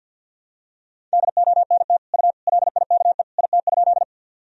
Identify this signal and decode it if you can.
Morse code, heard as a keyed tone, about 46 wpm, 700 Hz. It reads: DONT U LIKE IT?
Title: Sentences from Top 300 Words and Abbreviations 50wpm